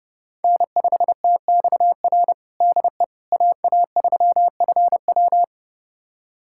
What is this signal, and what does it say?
Morse code, heard as a keyed tone, about 30 wpm, 700 Hz. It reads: N5TXR DE AA3FW
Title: USA Callsign DE Callsign - 5 Characters - 30wpm